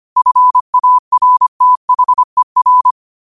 Morse code, heard as a keyed tone, about 25 wpm, 1000 Hz. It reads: FARTHER